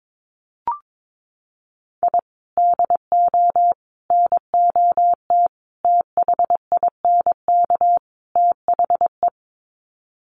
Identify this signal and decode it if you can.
Morse code, heard as a keyed tone, about 22 wpm, 700 Hz. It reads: I DO NOT THINK THE
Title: Sets of 5 Words 22wpm